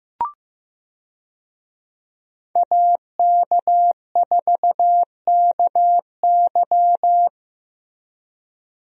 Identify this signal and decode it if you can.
Morse code, heard as a keyed tone, about 15 wpm, 700 Hz. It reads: AK4KY
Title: USA Call Signs - 5 Characters - Repeated 3x - 15wpm